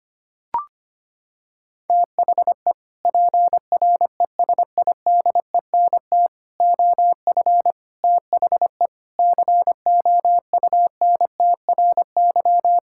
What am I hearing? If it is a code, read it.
Morse code, heard as a keyed tone, about 25 wpm, 700 Hz. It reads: THE PRESIDENT OF THE COUNTRY
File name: practice / 25wpm / Sentences with 3-Word Phrases - 25wpm